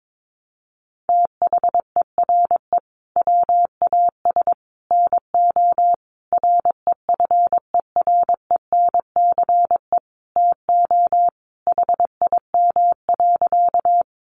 Morse code, heard as a keyed tone, about 22 wpm, 700 Hz. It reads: THERE WAS NO REFERENCE TO HIM.
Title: Sentences from Top 1200 Words 22wpm